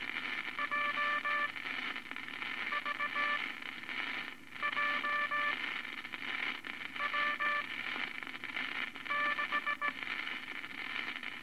Morse code, heard as a keyed tone, about 18 wpm, spaced as for 6 wpm, 1170 Hz, with noise. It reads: JVJW6
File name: Morse Code Records / 15B LCM 159 Units I-V Groups of Five Cont